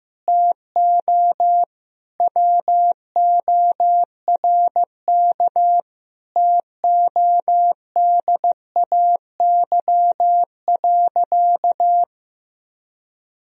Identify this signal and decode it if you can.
Morse code, heard as a keyed tone, about 16 wpm, 700 Hz. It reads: TO WORK TODAY.
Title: Sentences from Top 1000 Words - Review Part 4 - 15wpm